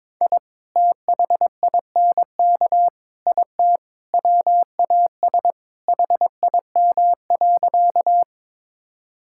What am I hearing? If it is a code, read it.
Morse code, heard as a keyed tone, about 22 wpm, 700 Hz. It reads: I THINK IT WAS HIM.
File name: practice / 22wpm / Sentences with 4-Word Phrases - 22wpm